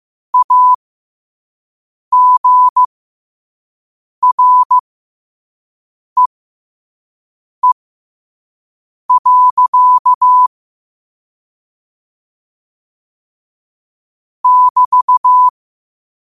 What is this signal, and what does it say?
Morse code, heard as a keyed tone, about 15 wpm, spaced as for 5 wpm, 1000 Hz. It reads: AGREE. <BT>
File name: qotdfiles / 5wpm